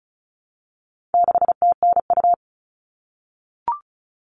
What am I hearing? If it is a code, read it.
Morse code, heard as a keyed tone, about 34 wpm, 700 Hz. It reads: 6TNU
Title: All Cut Numbers - 4-Digit 35wpm